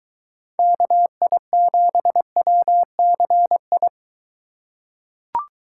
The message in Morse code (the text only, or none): KI7WCI